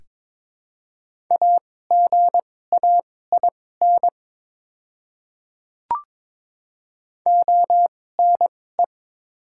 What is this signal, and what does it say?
Morse code, heard as a keyed tone, about 22 wpm, spaced as for 16 wpm, 700 Hz. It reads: AGAIN ONE
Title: 2x Character Spacing - Top 300 Words - 22wpm